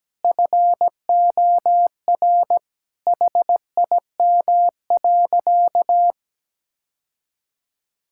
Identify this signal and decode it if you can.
Morse code, heard as a keyed tone, about 17 wpm, 700 Hz. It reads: FOR HIM.
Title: Sentences from Top 1000 Words - Review Part 6 - 17wpm